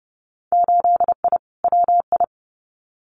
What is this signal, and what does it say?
Morse code, heard as a keyed tone, about 30 wpm, 700 Hz. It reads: 8I WI
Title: Winter Field Day Exchanges 30wpm